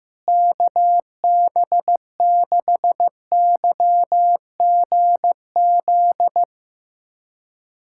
Morse code, heard as a keyed tone, about 15 wpm, 700 Hz. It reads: KB6YGZ
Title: USA Callsign DE Callsign - 6 Characters - 15wpm